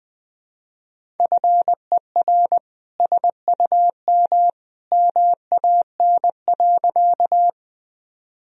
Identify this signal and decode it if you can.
Morse code, heard as a keyed tone, about 20 wpm, 700 Hz. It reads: FER SUM MAN.
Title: Sentences from Top 200 Words and Abbreviations 20wpm